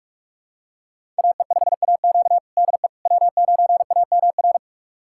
Morse code, heard as a keyed tone, about 46 wpm, 700 Hz. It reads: AE5AQ DE W9AMR
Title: USA Callsign DE Callsign - 5 Characters - 45wpm